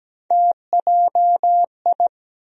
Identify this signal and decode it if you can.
Morse code, heard as a keyed tone, about 17 wpm, 700 Hz. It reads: TJI
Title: USA Call Sign Postfixes - 3 Letters 17wpm